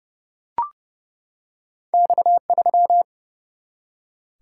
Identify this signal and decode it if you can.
Morse code, heard as a keyed tone, about 30 wpm, 700 Hz. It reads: X3
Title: Single-Letter Single-Number 30wpm